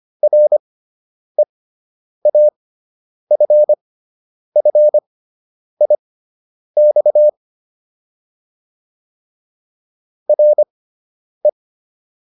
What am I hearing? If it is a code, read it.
Morse code, heard as a keyed tone, about 25 wpm, spaced as for 9 wpm, 600 Hz. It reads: REAFFIX RE